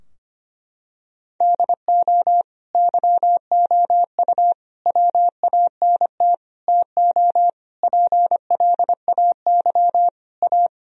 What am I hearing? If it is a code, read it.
Morse code, heard as a keyed tone, about 25 wpm, 700 Hz. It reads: DO YOU WANT TO PLAY A
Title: Sentences from Top 1000 Words - Review Part 3 - 25wpm